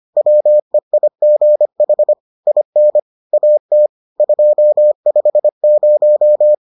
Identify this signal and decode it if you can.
Morse code, heard as a keyed tone, about 25 wpm, 600 Hz. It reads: WEIGH IN AT 250